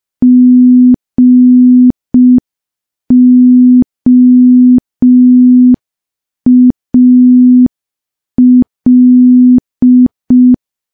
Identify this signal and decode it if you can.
Morse code, heard as a keyed tone, about 5 wpm, 250 Hz. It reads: GOAL
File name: morse / morse-training-05WPM